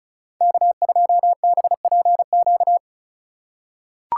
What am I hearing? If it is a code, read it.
Morse code, heard as a keyed tone, about 36 wpm, 700 Hz. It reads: K2BPQ
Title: USA Call Signs - 5 Characters - 2x Character Spacing - 35wpm